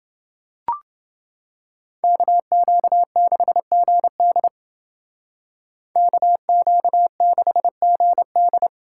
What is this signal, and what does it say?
Morse code, heard as a keyed tone, about 28 wpm, 700 Hz. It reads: KQ6GD KQ6GD